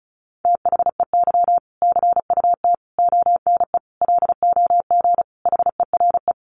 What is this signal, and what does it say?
Morse code, heard as a keyed tone, about 34 wpm, 700 Hz. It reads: THEY CUT ONE LOG HERE
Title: Sentences from Top 1400 Words 35wpm